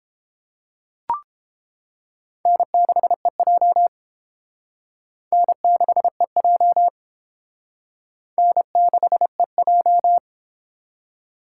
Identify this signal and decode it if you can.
Morse code, heard as a keyed tone, about 30 wpm, 700 Hz. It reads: N6EJ N6EJ N6EJ